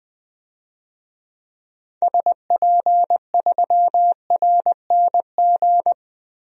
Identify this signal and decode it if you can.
Morse code, heard as a keyed tone, about 20 wpm, 700 Hz. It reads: SP3RNG
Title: DXCC-100 Call Signs - 6 Characters - Repeated 3x - 20wpm